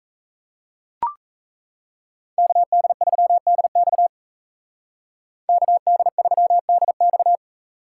Morse code, heard as a keyed tone, about 36 wpm, 700 Hz. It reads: KD3DX KD3DX